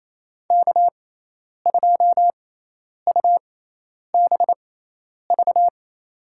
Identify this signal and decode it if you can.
Morse code, heard as a keyed tone, about 28 wpm, spaced as for 10 wpm, 700 Hz. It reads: K2UBV
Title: USA Call Signs - 5 Characters - 6x Character Spacing - 28wpm